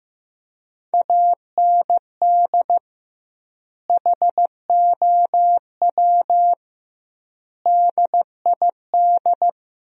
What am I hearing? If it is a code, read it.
Morse code, heard as a keyed tone, about 15 wpm, 700 Hz. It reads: AND HOW DID
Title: 2x Word-Spacing - Sentences from Top 200 Words - 15wpm